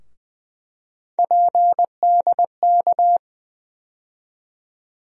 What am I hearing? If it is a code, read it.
Morse code, heard as a keyed tone, about 20 wpm, 700 Hz. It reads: PDK
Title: USA Call Sign Postfixes - 3 Letters 20wpm